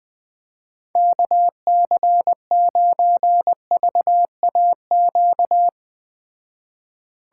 Morse code, heard as a keyed tone, about 20 wpm, 700 Hz. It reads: KC9VAQ